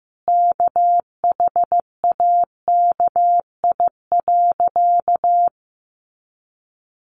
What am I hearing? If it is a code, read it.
Morse code, heard as a keyed tone, about 15 wpm, 700 Hz. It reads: KHAKI.
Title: Additional Easy Phrases 15wpm